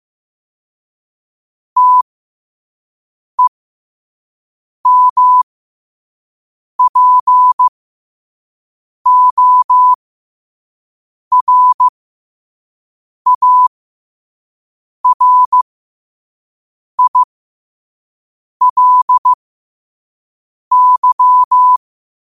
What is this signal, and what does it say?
Morse code, heard as a keyed tone, about 15 wpm, spaced as for 5 wpm, 1000 Hz. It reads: TEMPORARILY
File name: qotdfiles / 5wpm